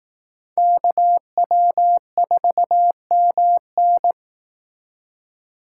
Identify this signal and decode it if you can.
Morse code, heard as a keyed tone, about 18 wpm, 700 Hz. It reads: KW4MN
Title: Speed Racing - USA 2x2 Call Signs 15wpm